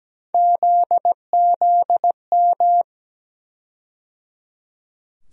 Morse code, heard as a keyed tone, about 17 wpm, 700 Hz. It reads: ZZM
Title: USA Call Sign Postfixes - 3 Letters 17wpm